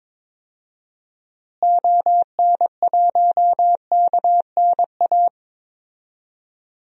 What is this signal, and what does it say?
Morse code, heard as a keyed tone, about 22 wpm, 700 Hz. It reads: ON1KNA
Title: DXCC-100 Call Signs - 6 Characters - Repeated 3x - 22wpm